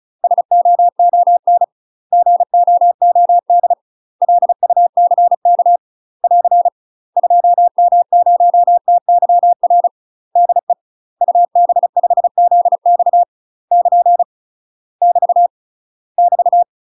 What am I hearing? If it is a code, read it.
Morse code, heard as a keyed tone, about 34 wpm, 700 Hz. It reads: SOON GOOD LUCK <AR> 2M0TYR DE UB5ZX <KN> <BT> <BT>
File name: UK MASTER QSOs TRAINING / 35WPM / UK MASTER QSOs 0024